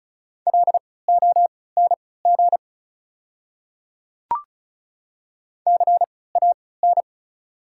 Morse code, heard as a keyed tone, about 36 wpm, spaced as for 20 wpm, 690 Hz. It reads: LONG CAN
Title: 3x Character Spacing - Top 100 Words - 35wpm